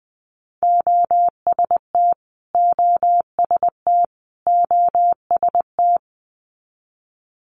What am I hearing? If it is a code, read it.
Morse code, heard as a keyed tone, about 19 wpm, 700 Hz. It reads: OST OST OST
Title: Multi-letter Phonemes Repeated 3x 20wpm